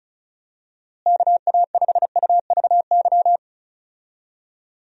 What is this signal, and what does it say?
Morse code, heard as a keyed tone, about 34 wpm, 700 Hz. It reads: KA5UVY
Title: USA Call Signs - 6 Characters - Repeated 2x - 35wpm